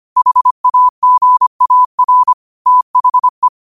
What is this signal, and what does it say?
Morse code, heard as a keyed tone, about 25 wpm, 1000 Hz. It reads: HAGAR THE